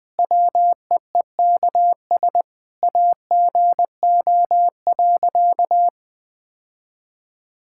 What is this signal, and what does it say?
Morse code, heard as a keyed tone, about 20 wpm, 700 Hz. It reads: WEEKS AGO.
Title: Sentences with 5-Word Phrases - 20wpm